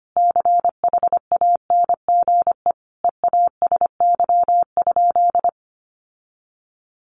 Morse code, heard as a keyed tone, about 25 wpm, 700 Hz. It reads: CHANGE EASY?
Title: Sentences from Top 1500 Words 25wpm